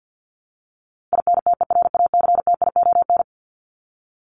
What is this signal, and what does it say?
Morse code, heard as a keyed tone, about 50 wpm, 700 Hz. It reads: INTERACTION
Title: Top 1251-1500 Words 50wpm